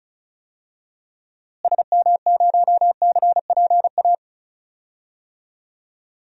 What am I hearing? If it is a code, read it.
Morse code, heard as a keyed tone, about 36 wpm, 700 Hz. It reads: SM0CPA
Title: DXCC-100 Call Signs - 6 Characters - Repeated 3x - 35wpm